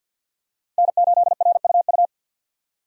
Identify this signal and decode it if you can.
Morse code, heard as a keyed tone, about 50 wpm, 700 Hz. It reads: N8RUU
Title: USA Call Signs - 5 Characters - 50wpm